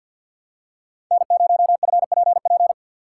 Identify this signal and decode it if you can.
Morse code, heard as a keyed tone, about 52 wpm, 700 Hz. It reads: N0FPP
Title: USA Call Signs - 3x Character Spacing - 50wpm